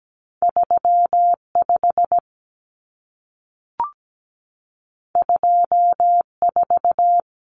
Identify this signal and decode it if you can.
Morse code, heard as a keyed tone, about 17 wpm, 700 Hz. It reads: 35 24
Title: Common Cut Numbers - 2-Digit 17wpm